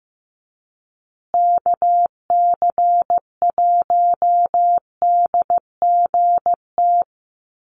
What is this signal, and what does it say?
Morse code, heard as a keyed tone, about 15 wpm, 700 Hz. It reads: KC1DGT
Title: USA Call Signs Repeated 3x 15wpm